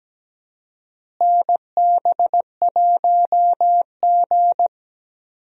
Morse code, heard as a keyed tone, about 17 wpm, 700 Hz. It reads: NB1G
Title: USA Call Signs - 4 Characters - Repeated 3x - 17wpm